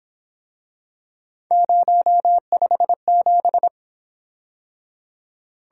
Morse code, heard as a keyed tone, about 28 wpm, 700 Hz. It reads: 057
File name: Speed-Racing - Three Letter-Number Combinations - 22wpm